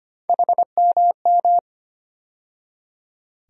Morse code, heard as a keyed tone, about 25 wpm, 700 Hz. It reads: HMM
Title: USA Call Sign Postfixes - 3 Letters 25wpm